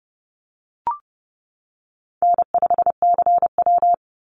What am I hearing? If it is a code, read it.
Morse code, heard as a keyed tone, about 30 wpm, 700 Hz. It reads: N5CW
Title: Easy Call Signs - 30wpm